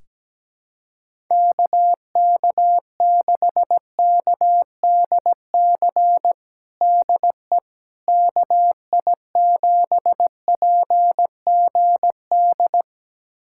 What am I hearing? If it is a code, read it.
Morse code, heard as a keyed tone, about 17 wpm, 700 Hz. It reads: KK6KDC DE KI7PGD
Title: USA Callsign DE Callsign 17wpm